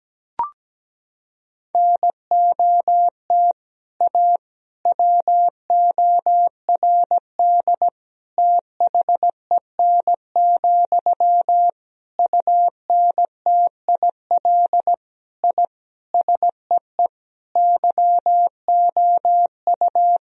Morse code, heard as a keyed tone, about 17 wpm, 700 Hz. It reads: NOT A WORD THEN, UNTIL I SEE YOU
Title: Sentences from Top 1000 Words - Review Part 1 - 17wpm